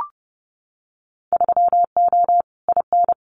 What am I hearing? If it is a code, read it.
Morse code, heard as a keyed tone, about 30 wpm, 700 Hz. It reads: 3O IN
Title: Winter Field Day Exchanges Encore 30wpm